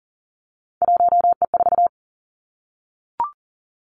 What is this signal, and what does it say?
Morse code, heard as a keyed tone, about 40 wpm, 700 Hz. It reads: 1E4
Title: Three Challenging Letter-Number Combinations 40wpm